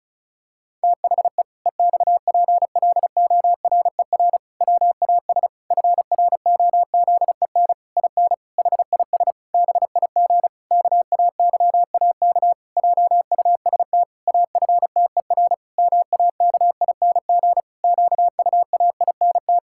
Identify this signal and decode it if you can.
Morse code, heard as a keyed tone, about 34 wpm, 700 Hz. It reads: THE EXPLORER WAS FROZEN IN HIS BIG KAYAK JUST AFTER MAKING QUAINT
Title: Pangram Challenge v2 - 35wpm